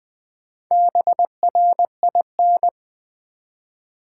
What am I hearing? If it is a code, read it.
Morse code, heard as a keyed tone, about 20 wpm, 700 Hz. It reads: BRIN
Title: Build A Word - 1 Letter At A Time - 3 to 5-Letter Words - 20wpm